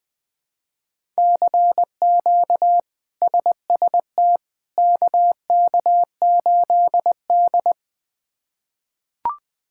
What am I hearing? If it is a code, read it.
Morse code, heard as a keyed tone, about 20 wpm, 700 Hz. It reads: CQ SST KK8D
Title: K1USN SST QSO Practice - 20wpm